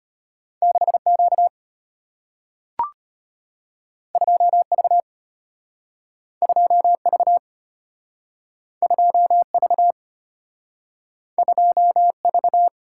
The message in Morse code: 6Q 2V 2V 2V 2V

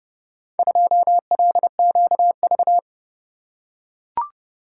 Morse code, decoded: 2LQV

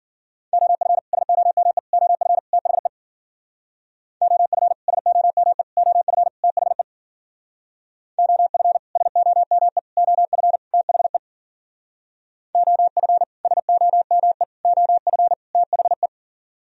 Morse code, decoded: OF SOME OF THE OF SOME OF THE OF SOME OF THE OF SOME OF THE

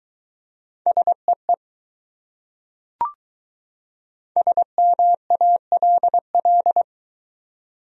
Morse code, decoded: SEE SMALL

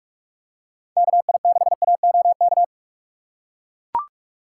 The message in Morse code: KI6AOK

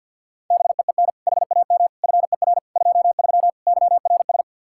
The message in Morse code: BEEN HAM FER 23 YRS

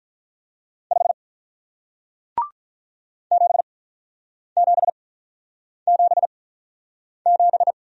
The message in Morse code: 5 7777

